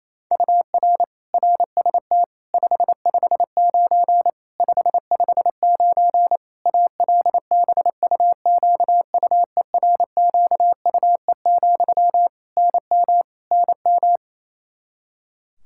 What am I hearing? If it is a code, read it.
Morse code, heard as a keyed tone, about 28 wpm, 700 Hz. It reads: UR RST 559 559 ALBUQUERQUE, NM NM
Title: Practice QSO - Exchange 1 - Beginning - 28wpm